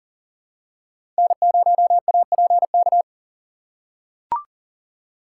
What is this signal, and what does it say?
Morse code, heard as a keyed tone, about 40 wpm, 700 Hz. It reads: N0APK
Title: USA Call Signs - 5 Characters - 3x Character Spacing - 40wpm